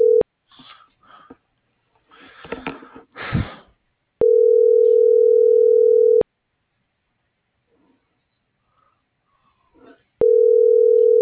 Ringback tone (North America), 440 + 480 Hz.